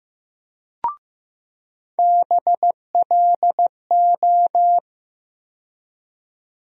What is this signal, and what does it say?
Morse code, heard as a keyed tone, about 15 wpm, 700 Hz. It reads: BLO
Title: USA Call Sign Postfixes - 3 Letters 15wpm